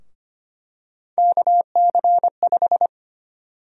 Morse code, heard as a keyed tone, about 25 wpm, 700 Hz. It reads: KC5
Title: USA Call Sign Prefixes - 25wpm